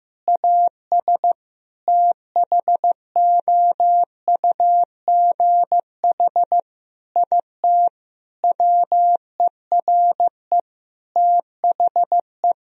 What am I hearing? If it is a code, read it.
Morse code, heard as a keyed tone, about 15 wpm, 700 Hz. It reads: AS THOUGH IT WERE THE
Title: Sentences from Top 1000 Words - Review Part 3 - 15wpm